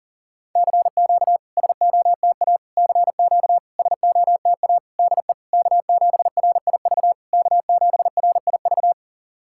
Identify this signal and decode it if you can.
Morse code, heard as a keyed tone, about 40 wpm, 700 Hz. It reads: CQ SOTA CQ SOTA DE K7RIV K7RIV
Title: SOTA QSO Practice 40wpm